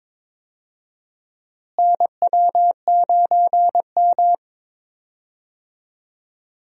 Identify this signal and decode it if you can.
Morse code, heard as a keyed tone, about 22 wpm, 700 Hz. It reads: NW9M
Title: USA Call Signs - 4 Characters - Repeated 2x - 22wpm